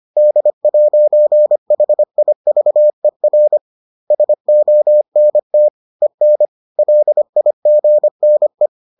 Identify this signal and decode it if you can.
Morse code, heard as a keyed tone, about 25 wpm, 600 Hz. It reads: D'HIVER SONT EN LIGNE
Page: bulletin UFT sous forme texte et en CW